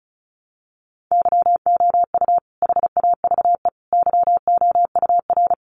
Morse code, heard as a keyed tone, about 34 wpm, 700 Hz. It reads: YOU HAVE YOUR
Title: Sentences from Top 400 Words 35wpm